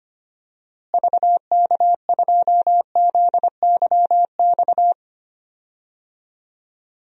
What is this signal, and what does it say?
Morse code, heard as a keyed tone, about 25 wpm, 700 Hz. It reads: VK2ZYX
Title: DXCC-100 Call Signs - 6 Characters - Repeated 3x - 25wpm